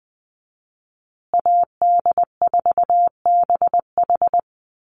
Morse code, heard as a keyed tone, about 20 wpm, 700 Hz. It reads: AD4BH
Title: USA Call Signs Repeated 2x 20wpm